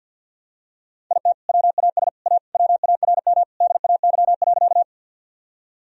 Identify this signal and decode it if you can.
Morse code, heard as a keyed tone, about 50 wpm, 700 Hz. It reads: IT WAS A WARM DAY.
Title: Sentences from Top 1000 Words - Review Part 2 - 50wpm